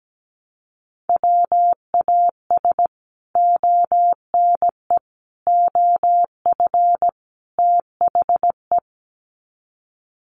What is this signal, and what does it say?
Morse code, heard as a keyed tone, about 17 wpm, 700 Hz. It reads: WAS ONE OF THE
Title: Sets of 4 Words 17wpm